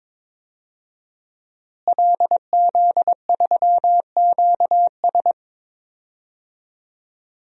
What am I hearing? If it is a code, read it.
Morse code, heard as a keyed tone, about 22 wpm, 700 Hz. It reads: LZ3QS
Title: DXCC-100 Call Signs - 5 Characters - Repeated 3x - 22wpm